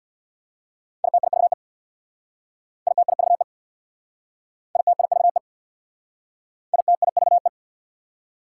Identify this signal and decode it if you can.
Morse code, heard as a keyed tone, about 66 wpm, 700 Hz. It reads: ITIVE ITIVE ITIVE ITIVE